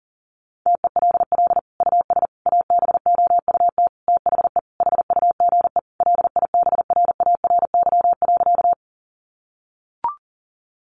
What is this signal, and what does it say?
Morse code, heard as a keyed tone, about 40 wpm, 700 Hz. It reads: TELL US ABOUT THE HUGE LIBRARY.
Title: Sentences from Top 1100 Words 40wpm